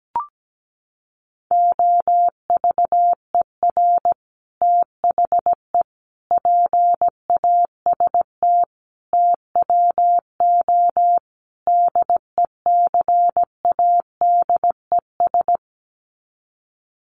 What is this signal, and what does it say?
Morse code, heard as a keyed tone, about 17 wpm, 700 Hz. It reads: OVER THE PAST TWO DECADES
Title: Sets of 5 Words Encore2 17wpm